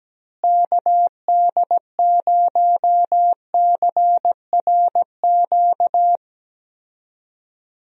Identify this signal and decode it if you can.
Morse code, heard as a keyed tone, about 17 wpm, 700 Hz. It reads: KD0CRQ
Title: USA Call Signs - 6 Characters - Repeated 3x - 17wpm